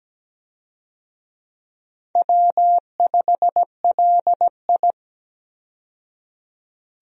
Morse code, heard as a keyed tone, about 17 wpm, 700 Hz. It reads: W5LI
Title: USA Call Signs - 4 Characters - 17wpm